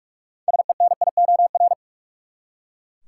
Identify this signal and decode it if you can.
Morse code, heard as a keyed tone, about 41 wpm, 700 Hz. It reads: SENIOR
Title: Six Letter Words - 45wpm